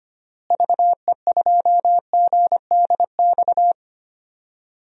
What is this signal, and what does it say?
Morse code, heard as a keyed tone, about 25 wpm, 700 Hz. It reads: VE2GDX
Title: DXCC-100 Call Signs - 6 Characters - Repeated 3x - 25wpm